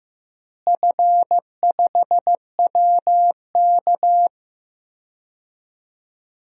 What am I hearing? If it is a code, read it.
Morse code, heard as a keyed tone, about 15 wpm, 700 Hz. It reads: F5WK
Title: DXCC-100 Call Signs - 4 Characters - Repeated 2x - 15wpm